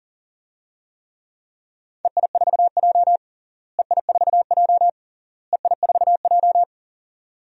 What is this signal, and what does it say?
Morse code, heard as a keyed tone, about 40 wpm, 700 Hz. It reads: EI4J EI4J EI4J
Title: DXCC-100 Call Signs - 4 Characters - Repeated 3x - 40wpm